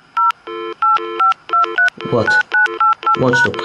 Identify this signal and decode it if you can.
DTMF, heard as telephone touch tones, tones of about 140 ms, with speech. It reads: 0 85239988#